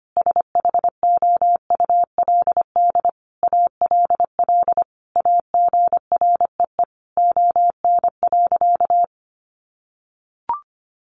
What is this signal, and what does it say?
Morse code, heard as a keyed tone, about 25 wpm, 700 Hz. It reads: SHOULD ALL AGREE ON.
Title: Sentences from Top 1100 Words 25wpm